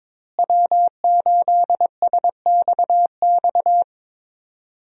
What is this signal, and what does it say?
Morse code, heard as a keyed tone, about 22 wpm, 700 Hz. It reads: W8SXX
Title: USA Call Signs - 5 Characters - Repeated 3x - 22wpm